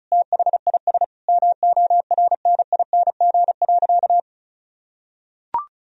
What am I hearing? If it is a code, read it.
Morse code, heard as a keyed tone, about 34 wpm, 700 Hz. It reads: THIS MORNING.
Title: Sentences from Top 1000 Words - Review Part 4 - 35wpm